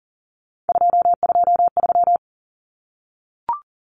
Morse code, heard as a keyed tone, about 40 wpm, 700 Hz. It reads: 223
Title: Common Cut Numbers - 3-Digit 40wpm